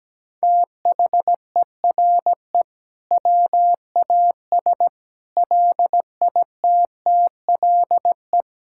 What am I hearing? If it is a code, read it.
Morse code, heard as a keyed tone, about 17 wpm, 700 Hz. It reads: THERE WAS LITTLE
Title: Sentences from Top 1000 Words - Review Part 5 - 17wpm